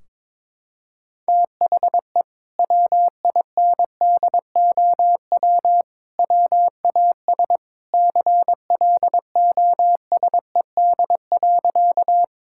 Morse code, heard as a keyed tone, about 22 wpm, 700 Hz. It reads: THE WINDOW WAS CLOSED.
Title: Sentences from Top 1600 Words 22wpm